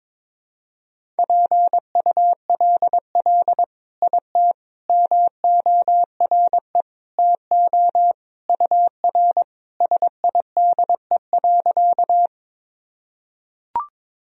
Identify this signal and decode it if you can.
Morse code, heard as a keyed tone, about 22 wpm, 700 Hz. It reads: PULL IT MORE TO UR SIDE.
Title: Sentences from Top 400 Words and Abbreviations 22wpm